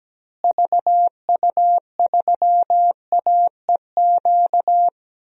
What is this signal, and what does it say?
Morse code, heard as a keyed tone, about 17 wpm, 700 Hz. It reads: VU3AEQ
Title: DXCC-100 Call Signs - 6 Characters - Repeated 3x - 17wpm